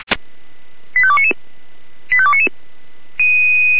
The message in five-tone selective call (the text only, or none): 85299 85299